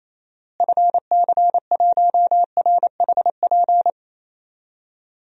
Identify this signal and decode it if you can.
Morse code, heard as a keyed tone, about 28 wpm, 700 Hz. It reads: FC1RHP
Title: DXCC-100 Call Signs - 6 Characters - Repeated 3x - 28wpm